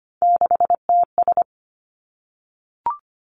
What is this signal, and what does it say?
Morse code, heard as a keyed tone, about 25 wpm, 700 Hz. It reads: 6TS
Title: Three Challenging Letter-Number Combinations 25wpm